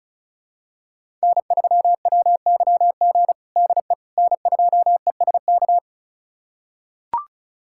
Morse code, heard as a keyed tone, about 34 wpm, 700 Hz. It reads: N3WYG DE N2ESK
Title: USA Callsign DE Callsign - 5 Characters - 35wpm